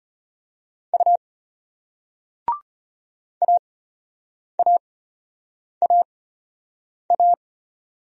Morse code, heard as a keyed tone, about 33 wpm, spaced as for 8 wpm, 700 Hz. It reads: U AAAA